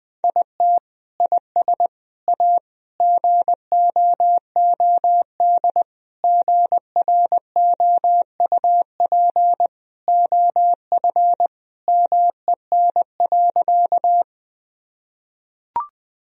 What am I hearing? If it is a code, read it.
Morse code, heard as a keyed tone, about 20 wpm, 700 Hz. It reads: IT IS A GOOD GROUP OF MEN.
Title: Sentences from Top 1000 Words - Review Part 1 - 20wpm